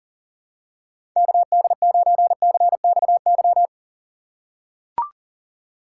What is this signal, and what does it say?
Morse code, heard as a keyed tone, about 40 wpm, 700 Hz. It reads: KD9CXY
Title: USA Call Signs - 6 Characters - 6x Character Spacing - 40wpm